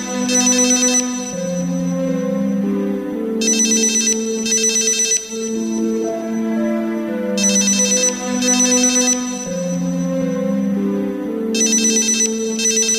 There is no code.